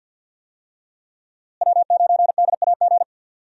Morse code, heard as a keyed tone, about 52 wpm, 700 Hz. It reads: W9DAG
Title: USA Call Signs - 5 Characters - Repeated 3x - 50wpm